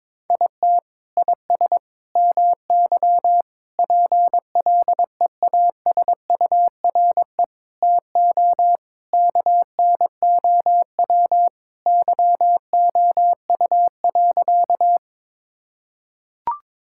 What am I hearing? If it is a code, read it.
Morse code, heard as a keyed tone, about 22 wpm, 700 Hz. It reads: IT IS MY PLEASURE TO KNOW YOU.
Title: Sentences from Top 1600 Words 22wpm